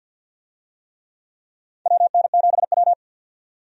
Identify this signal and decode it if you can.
Morse code, heard as a keyed tone, about 47 wpm, 700 Hz. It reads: WN7W
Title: USA Call Signs - 4 Characters - 50wpm